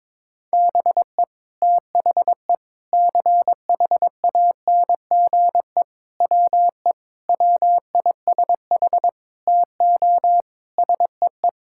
Morse code, heard as a keyed tone, about 22 wpm, 700 Hz. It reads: BE THE CHANGE WE WISH TO SEE
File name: practice / 22wpm / Sets of 8 Words - 22wpm